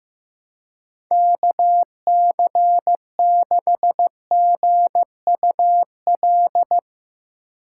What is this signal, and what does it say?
Morse code, heard as a keyed tone, about 15 wpm, 700 Hz. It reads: KC6GUL
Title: Speed Racing - USA Call Signs 15wpm